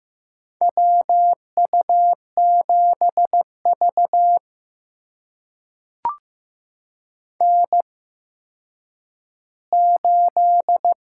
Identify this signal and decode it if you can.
Morse code, heard as a keyed tone, about 15 wpm, spaced as for 9 wpm, 700 Hz. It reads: WU7V N8
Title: USA Call Signs - 4 Characters - 8x Character Spacing - 15wpm